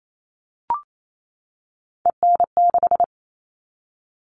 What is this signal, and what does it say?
Morse code, heard as a keyed tone, about 28 wpm, 700 Hz. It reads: EN6
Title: All Cut Numbers - 3-Digit 28wpm